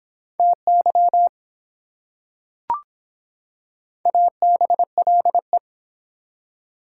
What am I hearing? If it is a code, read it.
Morse code, heard as a keyed tone, about 24 wpm, 700 Hz. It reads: TY ABLE